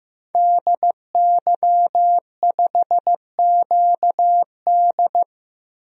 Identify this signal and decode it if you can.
Morse code, heard as a keyed tone, about 16 wpm, 700 Hz. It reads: DY5QD
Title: DXCC-100 Call Signs - 5 Characters - 15wpm